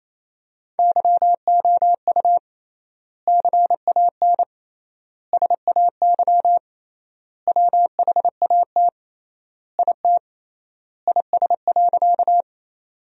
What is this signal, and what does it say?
Morse code, heard as a keyed tone, about 28 wpm, 700 Hz. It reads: YOU CAN SAY WHAT IT IS.
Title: 3x Word-Spacing - Sentences from Top 200 Words - 28wpm